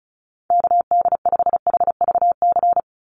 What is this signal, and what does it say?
Morse code, heard as a keyed tone, about 34 wpm, 700 Hz. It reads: KD5HVC
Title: USA Call Signs 35wpm